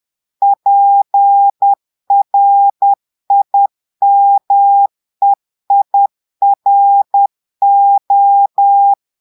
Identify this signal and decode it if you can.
Morse code, heard as a keyed tone, about 10 wpm, 800 Hz. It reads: PRIMEIRO
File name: Portugais - texte en clair 1 - vitesse - 10 mots minute